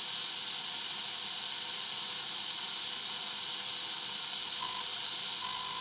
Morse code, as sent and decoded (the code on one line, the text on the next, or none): none